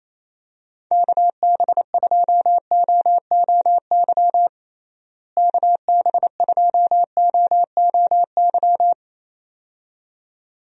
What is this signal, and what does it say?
Morse code, heard as a keyed tone, about 27 wpm, 700 Hz. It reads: KB2OOY KB2OOY
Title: USA Call Signs - 6 Characters - Repeated 2x - 28wpm